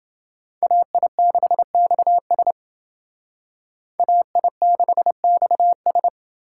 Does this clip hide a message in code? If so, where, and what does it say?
Yes: Morse code from 0.6 to 6.1 s, heard as a keyed tone, about 29 wpm, 700 Hz. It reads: AI6XS AI6XS